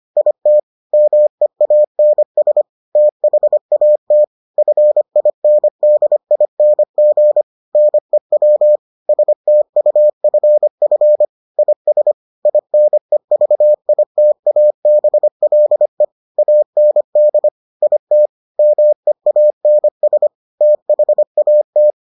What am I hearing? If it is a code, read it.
Morse code, heard as a keyed tone, about 25 wpm, 600 Hz. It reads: IT MEANS THAT FINDING NEW STUFF IS INEVITABLE AND IT MEANS THAT